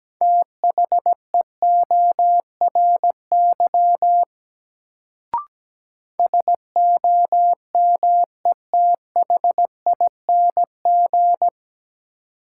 Morse code, heard as a keyed tone, about 17 wpm, 700 Hz. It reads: THEORY SOMETHING
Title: Sentences from Top 600 Words - New Word Drill - 17wpm